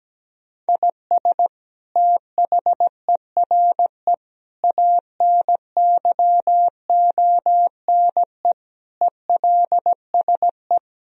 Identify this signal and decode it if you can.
Morse code, heard as a keyed tone, about 17 wpm, 700 Hz. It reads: IS THERE ANYONE ELSE
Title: Sentences from Top 1000 Words - Review Part 3 - 17wpm